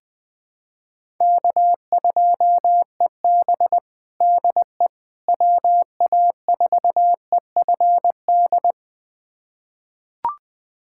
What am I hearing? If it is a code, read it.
Morse code, heard as a keyed tone, about 20 wpm, 700 Hz. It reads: K2EB DE WA4EFD
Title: USA Callsign DE Callsign 20wpm